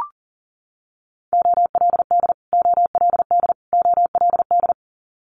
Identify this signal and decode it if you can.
Morse code, heard as a keyed tone, about 40 wpm, 700 Hz. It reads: OLD OLD OLD
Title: Top 300 Words Repeated 40wpm